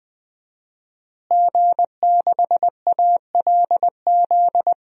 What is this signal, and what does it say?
Morse code, heard as a keyed tone, about 20 wpm, 700 Hz. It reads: G6ALZ
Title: DXCC-100 Call Signs - 5 Characters - Repeated 2x - 20wpm